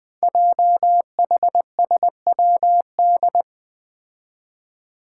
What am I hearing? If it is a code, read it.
Morse code, heard as a keyed tone, about 20 wpm, 700 Hz. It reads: JHSWD
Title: Code Groups - 5 Letters - 20wpm